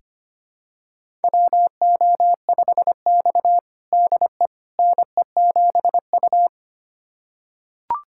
Morse code, heard as a keyed tone, about 25 wpm, 700 Hz. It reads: WO5X DE NE7U
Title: USA Callsign DE Callsign - 4 Characters - 25wpm